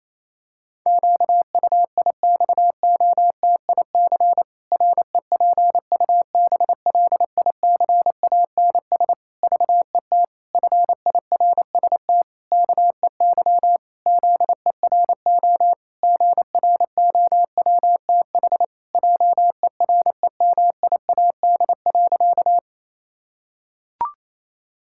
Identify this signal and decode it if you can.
Morse code, heard as a keyed tone, about 28 wpm, 700 Hz. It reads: QUIXOTIC REPUBLICANS VET FIRST KEY ZERO GROWTH JEREMIAD.
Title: Pangram Challenge v2 - 28wpm